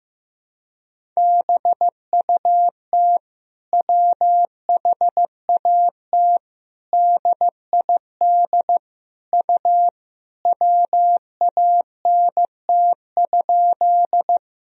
Morse code, heard as a keyed tone, about 16 wpm, 700 Hz. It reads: BUT WHAT DID U WANT?
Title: Sentences from Top 200 Words and Abbreviations 15wpm